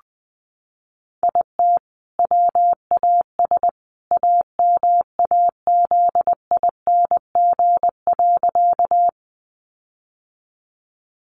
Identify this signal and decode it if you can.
Morse code, heard as a keyed tone, about 20 wpm, 700 Hz. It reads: IT WAS AMAZING.
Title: Sentences from Top 1500 Words 20wpm